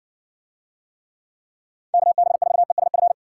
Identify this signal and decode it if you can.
Morse code, heard as a keyed tone, about 60 wpm, 700 Hz. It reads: KB4ESF